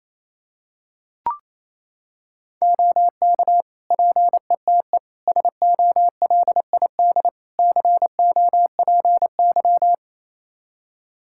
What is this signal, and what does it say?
Morse code, heard as a keyed tone, about 28 wpm, 700 Hz. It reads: OK PETE SOLID COPY
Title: Practice QSO - Exchange 3 - More Info - 28wpm